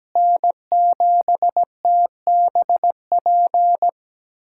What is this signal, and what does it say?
Morse code, heard as a keyed tone, about 17 wpm, 700 Hz. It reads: N7TBP
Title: USA Callsign DE Callsign - 5 Characters - 17wpm